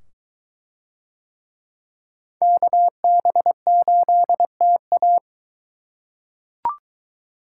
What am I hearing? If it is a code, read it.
Morse code, heard as a keyed tone, about 21 wpm, 700 Hz. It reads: KB8TA